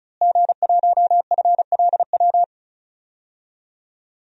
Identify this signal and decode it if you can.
Morse code, heard as a keyed tone, about 34 wpm, 700 Hz. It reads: G1FLW
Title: DXCC-100 Call Signs - 5 Characters - Repeated 3x - 35wpm